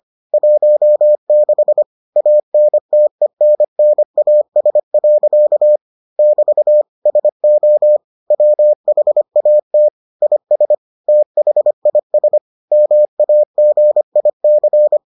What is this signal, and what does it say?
Morse code, heard as a keyed tone, about 25 wpm, 600 Hz. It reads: 16 ANTENNAS. <BT> SO WHAT IS THIS MAGIC